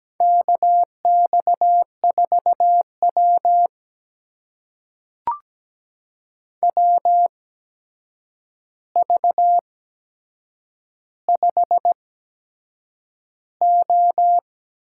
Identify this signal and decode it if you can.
Morse code, heard as a keyed tone, about 17 wpm, spaced as for 7 wpm, 700 Hz. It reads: KX4W WV5O
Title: USA Call Signs - 4 Characters - 8x Character Spacing - 17wpm